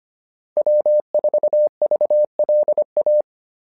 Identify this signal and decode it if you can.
Morse code, heard as a keyed tone, about 25 wpm, 600 Hz. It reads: W4VLA
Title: SOTA QSO Practice 25wpm